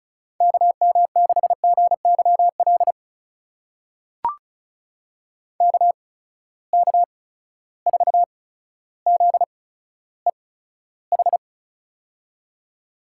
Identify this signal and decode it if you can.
Morse code, heard as a keyed tone, about 36 wpm, spaced as for 15 wpm, 700 Hz. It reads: KM6GYL KK4ZEH
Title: USA Call Signs - 6 Characters - 8x Character Spacing - 35wpm